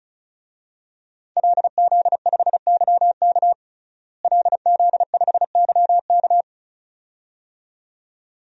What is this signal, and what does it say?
Morse code, heard as a keyed tone, about 34 wpm, 700 Hz. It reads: LZ5YK LZ5YK
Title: DXCC-100 Call Signs - 5 Characters - Repeated 2x - 35wpm